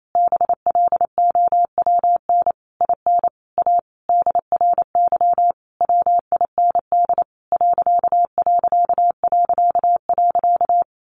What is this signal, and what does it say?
Morse code, heard as a keyed tone, about 28 wpm, 700 Hz. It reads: BLOWN IN A DRY WIND ....
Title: A Sound Of Thunder 28wpm